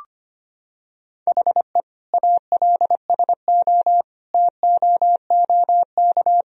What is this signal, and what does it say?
Morse code, heard as a keyed tone, about 25 wpm, 700 Hz. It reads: HE ALSO TOOK
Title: Sentences from Top 1000 Words - Review Part 1 - 25wpm